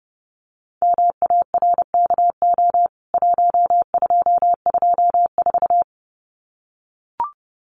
Morse code, heard as a keyed tone, about 30 wpm, 700 Hz. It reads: MARKO 1224
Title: CWT Exchange 30wpm